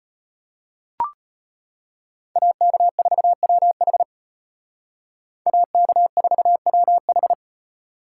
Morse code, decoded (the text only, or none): AK4WH AK4WH